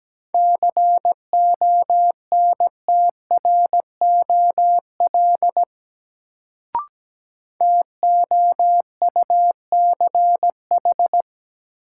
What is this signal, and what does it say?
Morse code, heard as a keyed tone, about 17 wpm, 700 Hz. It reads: CONTROL TOUCH
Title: Sentences from Top 1200 Words - New Word Drill Alt - 17wpm